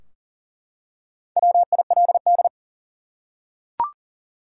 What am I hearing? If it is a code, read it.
Morse code, heard as a keyed tone, about 40 wpm, 700 Hz. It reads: WILD
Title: Four Letter Words - 40wpm